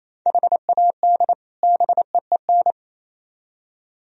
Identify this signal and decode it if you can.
Morse code, heard as a keyed tone, about 36 wpm, 700 Hz. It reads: HAD BEEN